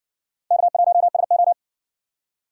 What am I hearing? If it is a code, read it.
Morse code, heard as a keyed tone, about 60 wpm, 700 Hz. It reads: K1SO